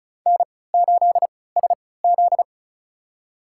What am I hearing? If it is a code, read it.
Morse code, heard as a keyed tone, about 36 wpm, spaced as for 20 wpm, 700 Hz. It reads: N8SZ
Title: USA Call Signs - 4 Characters - 3x Character Spacing - 35wpm